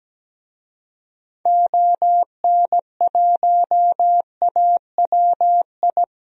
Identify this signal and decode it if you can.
Morse code, heard as a keyed tone, about 17 wpm, 700 Hz. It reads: ON1AWI
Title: DXCC-100 Call Signs - 6 Characters - 17wpm